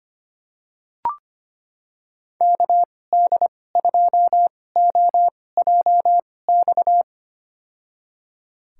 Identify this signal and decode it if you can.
Morse code, heard as a keyed tone, about 25 wpm, spaced as for 18 wpm, 700 Hz. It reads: KD2OJX
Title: USA Call Signs - 6 Characters - 2x Character Spacing - 25wpm